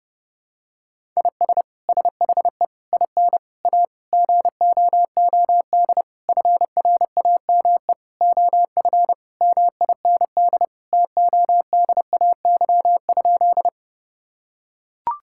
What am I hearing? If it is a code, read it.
Morse code, heard as a keyed tone, about 28 wpm, 700 Hz. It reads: IS SHE IN A GOOD FRAME OF MIND TODAY?
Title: Sentences from Top 1000 Words - Review Part 6 - 30wpm